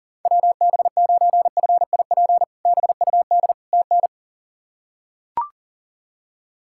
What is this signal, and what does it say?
Morse code, heard as a keyed tone, about 40 wpm, 700 Hz. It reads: WB9FIP BUD TN
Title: K1USN SST QSO Practice - 40wpm